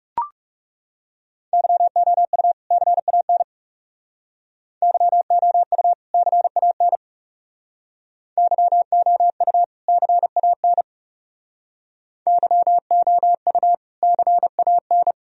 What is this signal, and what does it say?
Morse code, heard as a keyed tone, about 38 wpm, 700 Hz. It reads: YOU CAN YOU CAN YOU CAN YOU CAN